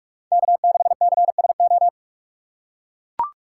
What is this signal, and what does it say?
Morse code, heard as a keyed tone, about 44 wpm, 700 Hz. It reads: K6CSO